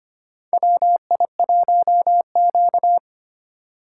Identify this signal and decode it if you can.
Morse code, heard as a keyed tone, about 25 wpm, 700 Hz. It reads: WI1Q
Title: USA Call Signs - 4 Characters - 2x Character Spacing - 25wpm